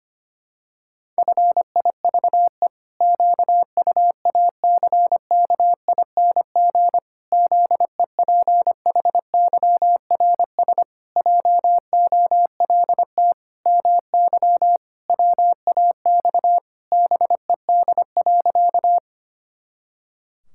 Morse code, heard as a keyed tone, about 25 wpm, 700 Hz. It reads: FIVE QUACKING ZEPHYRS JOLT MY WAX BED.
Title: Pangram Challenge v2 - 25wpm